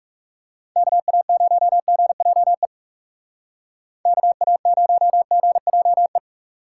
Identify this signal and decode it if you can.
Morse code, heard as a keyed tone, about 36 wpm, 700 Hz. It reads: KA0GJE KA0GJE
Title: Speed Racing - USA Call Signs 30wpm